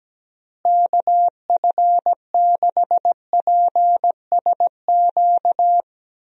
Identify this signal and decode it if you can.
Morse code, heard as a keyed tone, about 17 wpm, 700 Hz. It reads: KF6PSQ
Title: USA Call Signs - 6 Characters - Repeated 2x - 17wpm